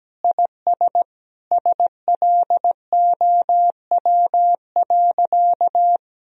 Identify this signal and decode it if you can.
Morse code, heard as a keyed tone, about 17 wpm, 700 Hz. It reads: IS SLOW.
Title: Sentences from Top 400 Words and Abbreviations 17wpm